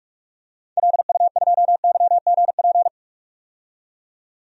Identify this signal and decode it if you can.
Morse code, heard as a keyed tone, about 46 wpm, 700 Hz. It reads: LU2YGP
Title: DXCC-100 Call Signs - 6 Characters - 45wpm